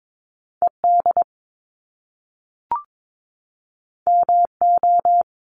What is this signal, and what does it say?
Morse code, heard as a keyed tone, about 22 wpm, 700 Hz. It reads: ED MO
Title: Top Two Letter Combinations in Words 22wpm